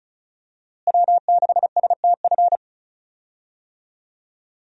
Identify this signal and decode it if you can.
Morse code, heard as a keyed tone, about 36 wpm, 700 Hz. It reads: W6STF
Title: USA Call Signs - 5 Characters - Repeated 2x - 35wpm